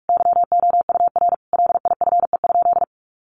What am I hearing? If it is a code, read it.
Morse code, heard as a keyed tone, about 44 wpm, 700 Hz. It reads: YOUR LIFE?
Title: Sentences from Top 900 Words 45wpm